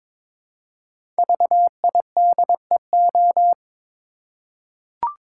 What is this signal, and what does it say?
Morse code, heard as a keyed tone, about 22 wpm, 700 Hz. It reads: VIDEO
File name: 5 Letter Words - First 3 Letters Given - 22wpm